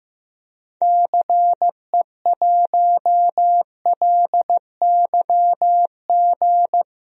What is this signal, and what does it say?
Morse code, heard as a keyed tone, about 16 wpm, 700 Hz. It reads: CE1LYG
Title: DXCC-100 Call Signs - 6 Characters - Repeated 3x - 15wpm